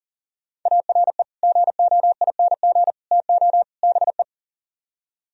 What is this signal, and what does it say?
Morse code, heard as a keyed tone, about 52 wpm, 700 Hz. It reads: ARE GOING TO BE